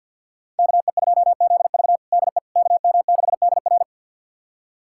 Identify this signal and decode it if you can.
Morse code, heard as a keyed tone, about 51 wpm, 700 Hz. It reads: KE2ZV DE KM6DR
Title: USA Callsign DE Callsign - 5 Characters - 50wpm